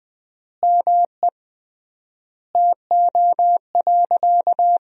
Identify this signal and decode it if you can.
Morse code, heard as a keyed tone, about 20 wpm, 700 Hz. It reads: ME TO.
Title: 3x Word-Spacing - Sentences from Top 400 Words - 20wpm